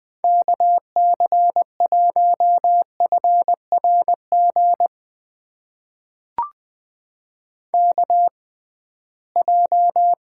Morse code, heard as a keyed tone, about 20 wpm, spaced as for 15 wpm, 700 Hz. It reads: KC1FRG KJ
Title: USA Call Signs - 6 Characters - 6x Character Spacing - 20wpm